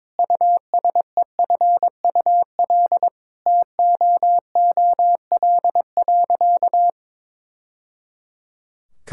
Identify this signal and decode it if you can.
Morse code, heard as a keyed tone, about 22 wpm, 700 Hz. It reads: USEFUL TOOL.
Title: Sentences from Top 1000 Words - Review Part 6 - 22wpm